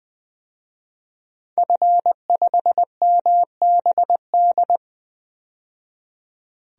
Morse code, heard as a keyed tone, about 20 wpm, 700 Hz. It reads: F5MBD
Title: DXCC-100 Call Signs - 5 Characters - 20wpm